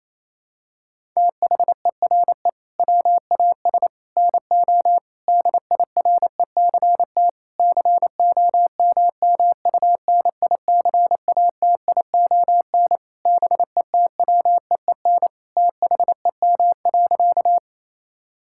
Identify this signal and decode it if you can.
Morse code, heard as a keyed tone, about 28 wpm, 700 Hz. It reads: THERE WAS NO DIRECT COMMUNICATION BETWEEN THEM.
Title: Sentences from Top 1000 Words - Review Part 6 - 28wpm